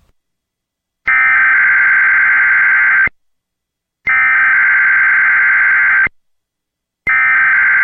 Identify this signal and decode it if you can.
Emergency Alert System (SAME) tones (header bursts), heard as harsh buzzing bursts. It is heard from 1.1 to 7.0 s.